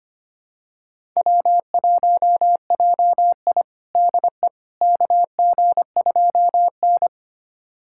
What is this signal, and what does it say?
Morse code, heard as a keyed tone, about 25 wpm, 700 Hz. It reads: W1JI DE KG2N
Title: USA Callsign DE Callsign - 4 Characters - 25wpm